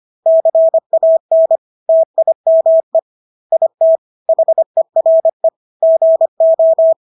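Morse code, heard as a keyed tone, about 25 wpm, 650 Hz. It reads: CAN TIME IT HERE GO
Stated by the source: Morse practice of English words at 25 wpm from CW med Gnister.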